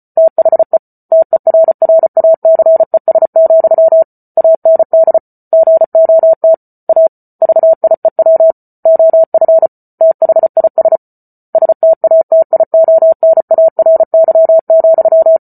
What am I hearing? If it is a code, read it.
Morse code, heard as a keyed tone, about 34 wpm, 650 Hz. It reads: THE TERRACES, AND GOT A VIEW OF THIS STATIONARY,